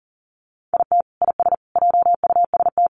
Morse code, heard as a keyed tone, about 40 wpm, 700 Hz. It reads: IT IS JUST
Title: Sentences from Top 600 Words 40wpm